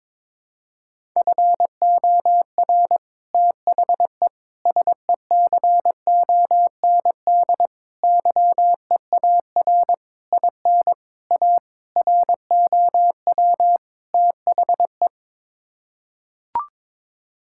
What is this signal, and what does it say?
Morse code, heard as a keyed tone, about 21 wpm, 700 Hz. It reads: FOR THE SECOND YEAR IN A ROW THE
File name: practice / 22wpm / Sets of 8 Words - 22wpm